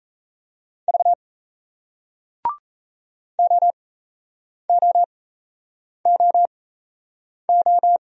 Morse code, heard as a keyed tone, about 37 wpm, spaced as for 8 wpm, 700 Hz. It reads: V OOOO